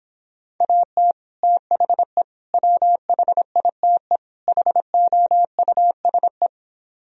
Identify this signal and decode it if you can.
Morse code, heard as a keyed tone, about 26 wpm, 700 Hz. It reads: AT THE WHITE HOUSE